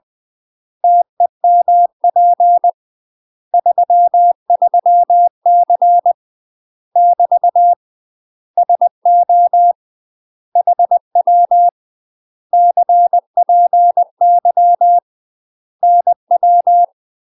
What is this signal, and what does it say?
Morse code, heard as a keyed tone, about 20 wpm, 700 Hz. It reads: TEMP 33C <BT> SO HW CPY NW
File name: UK MASTER QSOs TRAINING / 20WPM / UK MASTER QSOs 0065